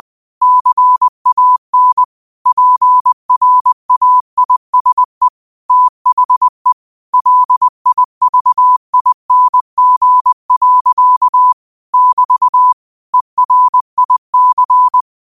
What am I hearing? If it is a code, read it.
Morse code, heard as a keyed tone, about 20 wpm, 1000 Hz. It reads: CAN PRAISE THE LIVING. <BT> ERIC
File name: qotdfiles / 20wpm